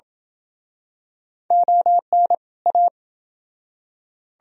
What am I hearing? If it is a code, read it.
Morse code, heard as a keyed tone, about 29 wpm, 700 Hz. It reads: ON A